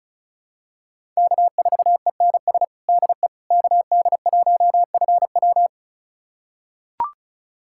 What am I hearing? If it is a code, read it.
Morse code, heard as a keyed tone, about 36 wpm, 700 Hz. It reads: K4ENS DE KD1FW
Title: USA Callsign DE Callsign 35wpm